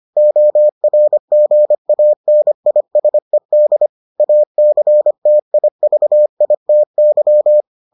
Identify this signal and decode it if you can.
Morse code, heard as a keyed tone, about 25 wpm, 600 Hz. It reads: ORGANISED ACTIVITY